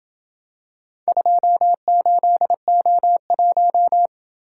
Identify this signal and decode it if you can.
Morse code, heard as a keyed tone, about 27 wpm, 700 Hz. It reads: 28O1